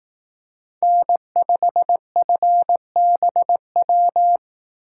Morse code, heard as a keyed tone, about 18 wpm, 700 Hz. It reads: N5FBW
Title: Speed Racing - USA 1x3 Call Signs 15wpm